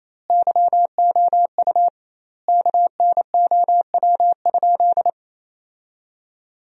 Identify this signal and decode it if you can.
Morse code, heard as a keyed tone, about 28 wpm, 700 Hz. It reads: YOU KNOW?
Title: 2x Word-Spacing - Sentences from Top 200 Words - 28wpm